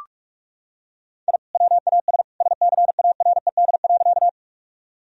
Morse code, heard as a keyed tone, about 41 wpm, 700 Hz. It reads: I WAS SCARED.
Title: Sentences from Top 1000 Words - Review Part 4 - 45wpm